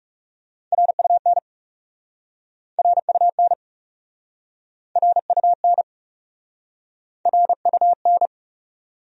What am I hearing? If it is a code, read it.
Morse code, heard as a keyed tone, about 40 wpm, 700 Hz. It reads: RUN RUN RUN RUN